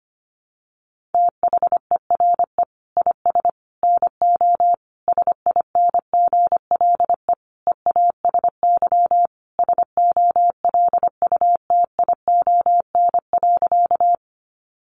Morse code, heard as a keyed tone, about 25 wpm, 700 Hz. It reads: THERE IS NO SINGLE EASY SOLUTION.
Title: Sentences from Top 1300 Words 25wpm